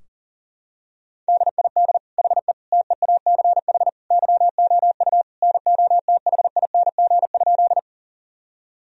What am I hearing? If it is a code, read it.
Morse code, heard as a keyed tone, about 37 wpm, 700 Hz. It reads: DID HE TEACH YOU NOTHING?
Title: Sentences from Top 1000 Words - Review Part 2 - 40wpm